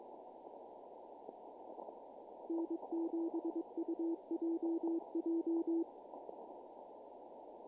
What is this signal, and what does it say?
Morse code, heard as a keyed tone, about 22 wpm, 330 Hz. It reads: N7UJJ
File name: contests / naqp-cw-k5tr-op-k5pi- -1700